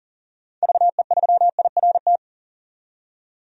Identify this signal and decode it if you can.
Morse code, heard as a keyed tone, about 40 wpm, 700 Hz. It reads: VE3IRT
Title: DXCC-100 Call Signs - 6 Characters - 40wpm